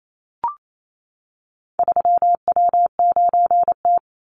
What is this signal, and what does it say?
Morse code, heard as a keyed tone, about 28 wpm, 700 Hz. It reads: 3W9T
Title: Call Signs - 28wpm